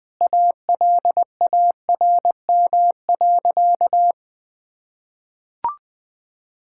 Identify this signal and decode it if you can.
Morse code, heard as a keyed tone, about 20 wpm, 700 Hz. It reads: ALARM.
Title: Sentences with 8-Word Phrases - 20wpm